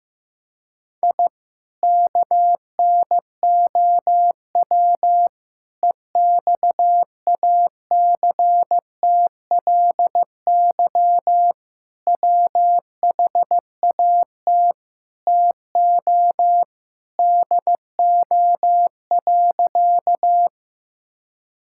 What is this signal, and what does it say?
Morse code, heard as a keyed tone, about 15 wpm, 700 Hz. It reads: I KNOW EXACTLY WHAT TO DO.
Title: Sentences from Top 1000 Words - Review Part 6 - 15wpm